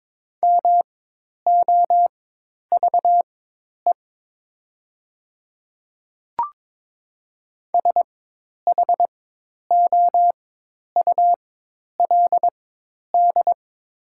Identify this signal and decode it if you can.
Morse code, heard as a keyed tone, about 22 wpm, spaced as for 10 wpm, 700 Hz. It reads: MOVE SHOULD
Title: 4x Character Spacing - Top 300 Words - 22wpm